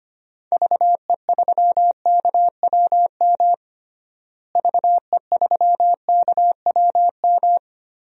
Morse code, heard as a keyed tone, about 25 wpm, 700 Hz. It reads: VE3KWM VE3KWM
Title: DXCC-100 Call Signs - 6 Characters - Repeated 3x - 25wpm